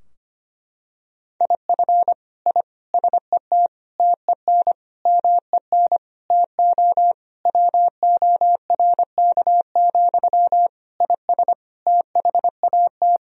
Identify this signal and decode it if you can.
Morse code, heard as a keyed tone, about 25 wpm, 700 Hz. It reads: IF I SET TEN MEN TO WORK, IS THAT
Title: Sentences from Top 1000 Words - Review Part 2 - 25wpm